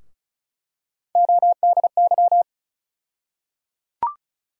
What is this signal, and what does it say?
Morse code, heard as a keyed tone, about 34 wpm, 700 Hz. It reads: ODY
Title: USA Call Sign Postfixes - 3 Letters 35wpm